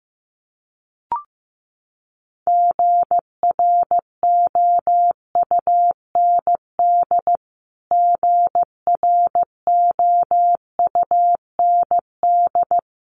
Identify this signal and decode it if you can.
Morse code, heard as a keyed tone, about 15 wpm, 700 Hz. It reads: GROUND GROUND
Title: Top 500 Words Repeated 15wpm